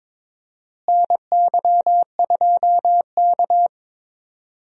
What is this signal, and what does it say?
Morse code, heard as a keyed tone, about 22 wpm, 700 Hz. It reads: NY2K
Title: USA Call Signs - 4 Characters - Repeated 3x - 22wpm